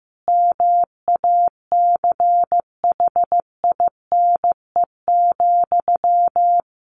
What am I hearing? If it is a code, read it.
Morse code, heard as a keyed tone, about 15 wpm, 700 Hz. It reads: MACHINE,
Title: A Sound Of Thunder 15wpm